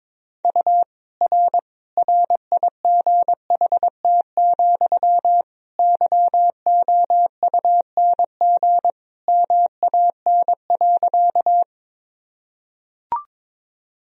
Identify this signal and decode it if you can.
Morse code, heard as a keyed tone, about 22 wpm, 700 Hz. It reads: U R RIGHT, YOUNG MAN.
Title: Sentences from Top 400 Words and Abbreviations 22wpm